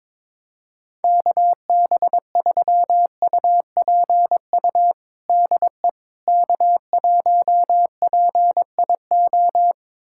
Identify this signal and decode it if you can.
Morse code, heard as a keyed tone, about 22 wpm, 700 Hz. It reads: KB3UPU DE K1PIO
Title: USA Callsign DE Callsign 22wpm